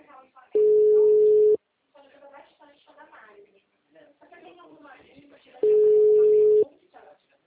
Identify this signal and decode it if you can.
Ringback tone (Europe, 425 Hz), with speech.